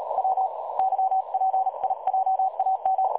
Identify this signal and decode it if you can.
Morse code, heard as a keyed tone, about 38 wpm, spaced as for 30 wpm, 760 Hz. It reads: WYPIJAM
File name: dlazaawansowanych / nagraniazpasma / websdr recording 3565.0kHz